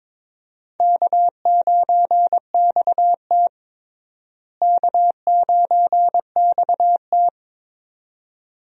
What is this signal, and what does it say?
Morse code, heard as a keyed tone, about 22 wpm, 700 Hz. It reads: K9XT K9XT
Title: USA Call Signs - 4 Characters - Repeated 2x - 22wpm